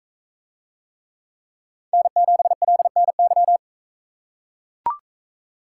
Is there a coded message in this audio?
Yes: Morse code from 1.9 to 3.6 s, heard as a keyed tone, about 41 wpm, 700 Hz. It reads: N7LNY